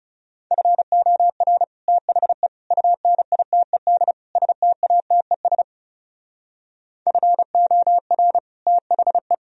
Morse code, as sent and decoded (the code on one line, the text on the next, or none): ..-. --- .-. / - .... / ..- -. .. - . -.. / ... - .- - . ... / ..-. --- .-. / - .... .
FOR THE UNITED STATES FOR THE